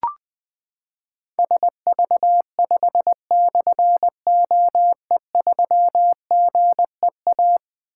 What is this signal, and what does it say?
Morse code, heard as a keyed tone, about 20 wpm, 700 Hz. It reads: SV5/OE3GEA
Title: Difficult Call Signs 20wpm